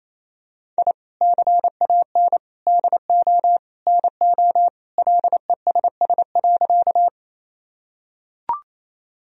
Morse code, heard as a keyed tone, about 28 wpm, 700 Hz. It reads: I CAN DO NO LESS.
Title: Sentences from Top 1000 Words - Review Part 2 - 28wpm